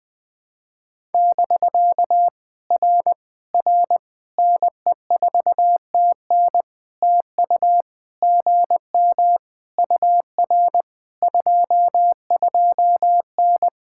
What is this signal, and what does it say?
Morse code, heard as a keyed tone, about 19 wpm, 700 Hz. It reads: <BK> R R NE4TN TU GM UR 22N
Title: POTA QSO Practice - 20wpm